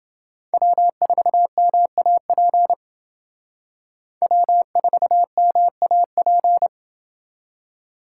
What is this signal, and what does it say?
Morse code, heard as a keyed tone, about 29 wpm, 700 Hz. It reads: W4MAP W4MAP